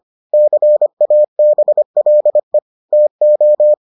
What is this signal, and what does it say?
Morse code, heard as a keyed tone, about 25 wpm, 600 Hz. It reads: CABLE TO